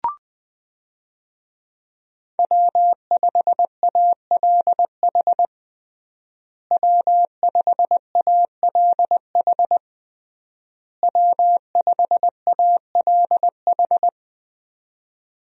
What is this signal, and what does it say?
Morse code, heard as a keyed tone, about 20 wpm, 700 Hz. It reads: W5ALH W5ALH W5ALH
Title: USA Call Signs - 5 Characters - Repeated 3x - 20wpm